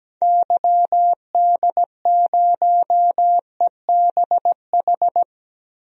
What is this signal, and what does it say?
Morse code, heard as a keyed tone, about 17 wpm, 700 Hz. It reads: YD0EBH
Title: DXCC-100 Call Signs - 6 Characters - 17wpm